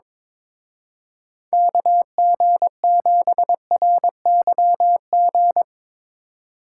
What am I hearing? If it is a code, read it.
Morse code, heard as a keyed tone, about 23 wpm, 700 Hz. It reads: KG7RYG
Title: Speed Racing - USA Call Signs 22wpm